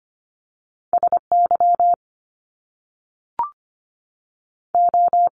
Morse code, heard as a keyed tone, about 25 wpm, 700 Hz. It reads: S Y O
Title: Top Two Letter Combinations in Words 25wpm